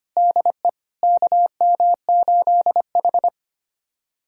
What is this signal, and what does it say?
Morse code, heard as a keyed tone, about 25 wpm, 700 Hz. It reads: DE KM8H
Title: USA Callsign DE Callsign - 4 Characters - 25wpm